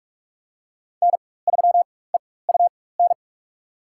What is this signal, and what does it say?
Morse code, heard as a keyed tone, about 42 wpm, spaced as for 20 wpm, 700 Hz. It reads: N3EUN
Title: USA Call Signs - 4x Character Spacing - 45wpm